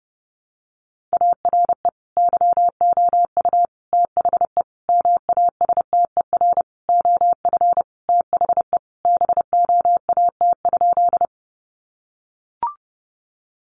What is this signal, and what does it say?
Morse code, heard as a keyed tone, about 30 wpm, 700 Hz. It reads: ARE YOU THE MASTER OF THE BOAT?
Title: Sentences from Top 1300 Words 30wpm